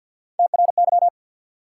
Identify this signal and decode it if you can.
Morse code, heard as a keyed tone, about 48 wpm, 700 Hz. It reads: TRY
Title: Three Letter Words - 50wpm